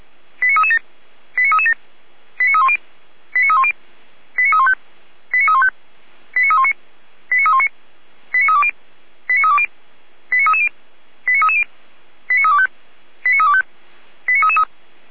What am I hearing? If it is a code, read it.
Five-tone selective call (ZVEI), heard as quick runs of five tones, tones of about 70 ms. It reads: 89337 89337 89310 89310 89315 89315 89319 89319 89320 89320 89330 89330 89325 89325 89333